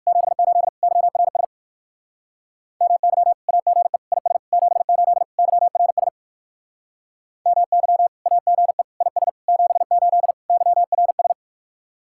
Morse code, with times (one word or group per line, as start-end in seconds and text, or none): 0.1-0.7 78
0.8-1.4 YRS
2.8-3.3 MY
3.5-4.0 AGE
4.1-4.4 IS
4.5-5.2 78
5.4-6.1 YRS
7.5-8.1 MY
8.3-8.8 AGE
9.0-9.3 IS
9.5-10.3 78
10.5-11.3 YRS